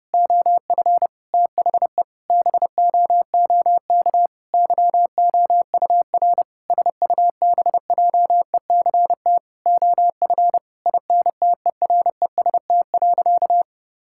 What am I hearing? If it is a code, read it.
Morse code, heard as a keyed tone, about 30 wpm, 700 Hz. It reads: OF THE BOOK YOUR SUBJECT OF INTEREST.
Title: Sentences from Top 1000 Words - Review Part 5 - 30wpm